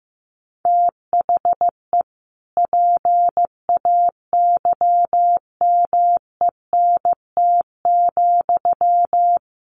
Morse code, heard as a keyed tone, about 15 wpm, 700 Hz. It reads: THE PAYMENT,
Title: Sentences from Top 1200 Words 15wpm